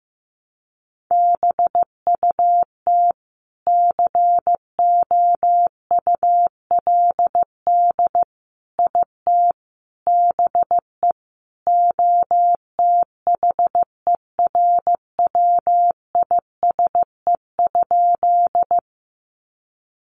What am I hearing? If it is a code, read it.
Morse code, heard as a keyed tone, about 15 wpm, 700 Hz. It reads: BUT COULD IT BE OTHERWISE?
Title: Sentences from Top 1200 Words 15wpm